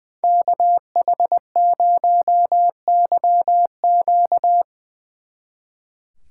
Morse code, heard as a keyed tone, about 21 wpm, 700 Hz. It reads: KH0YQ
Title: Speed Racing - USA 2x2 Call Signs 20wpm